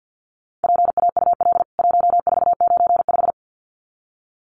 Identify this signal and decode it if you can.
Morse code, heard as a keyed tone, about 50 wpm, 700 Hz. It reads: PAUL 1495
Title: CWT Exchange 50wpm